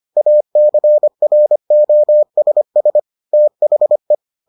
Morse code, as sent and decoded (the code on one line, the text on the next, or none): .- -.-. .-. --- ... ... / - .... .
ACROSS THE